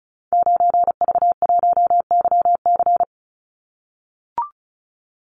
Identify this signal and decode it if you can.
Morse code, heard as a keyed tone, about 34 wpm, 700 Hz. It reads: 9V1YC
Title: Call Signs - 35wpm